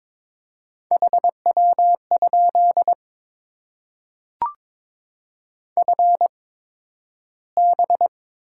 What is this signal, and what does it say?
Morse code, heard as a keyed tone, about 22 wpm, spaced as for 12 wpm, 700 Hz. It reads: HW? FB
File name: Top 20 QSO Elements - 8x Character Spacing - 22wpm